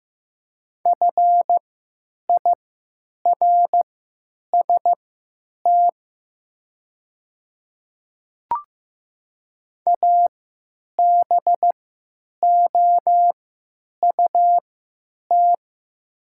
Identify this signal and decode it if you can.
Morse code, heard as a keyed tone, about 15 wpm, spaced as for 9 wpm, 700 Hz. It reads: FIRST ABOUT
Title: 3x Character Spacing - Top 100 Words - 15wpm